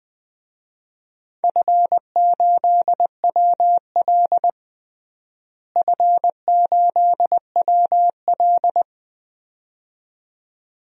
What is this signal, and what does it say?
Morse code, heard as a keyed tone, about 21 wpm, 700 Hz. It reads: F8WL F8WL
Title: DXCC-100 Call Signs - 4 Characters - Repeated 2x - 20wpm